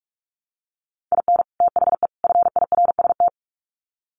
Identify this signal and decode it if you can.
Morse code, heard as a keyed tone, about 44 wpm, 700 Hz. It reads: IN THE FIRST
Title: Sets of 3 Words 45wpm